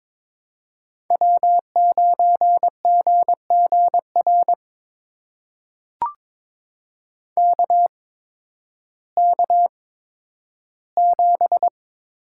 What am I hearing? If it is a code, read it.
Morse code, heard as a keyed tone, about 22 wpm, spaced as for 12 wpm, 700 Hz. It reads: W9GGR KK7
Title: USA Call Signs - 5 Characters - 8x Character Spacing - 22wpm